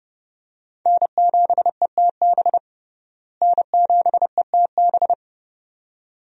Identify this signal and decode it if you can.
Morse code, heard as a keyed tone, about 32 wpm, 700 Hz. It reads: N7ETB N7ETB
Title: USA Call Signs - 5 Characters - Repeated 2x - 30wpm